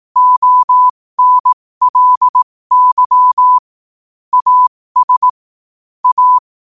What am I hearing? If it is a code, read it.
Morse code, heard as a keyed tone, about 18 wpm, spaced as for 15 wpm, 1000 Hz. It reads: ONLY AS A
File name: qotdfiles / 15wpm